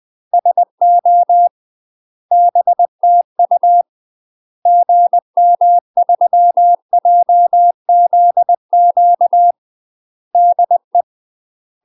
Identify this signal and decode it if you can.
Morse code, heard as a keyed tone, about 20 wpm, 700 Hz. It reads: SO BTU GM3JZQ DE
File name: UK MASTER QSOs TRAINING/20WPM